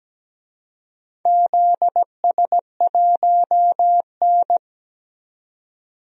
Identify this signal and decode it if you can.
Morse code, heard as a keyed tone, about 17 wpm, 700 Hz. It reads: ZS1N
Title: DXCC-100 Call Signs - 4 Characters - 17wpm